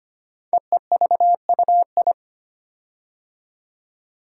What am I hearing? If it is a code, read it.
Morse code, heard as a keyed tone, about 25 wpm, 700 Hz. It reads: EEVUI
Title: Code Groups - 5 Letters - 25wpm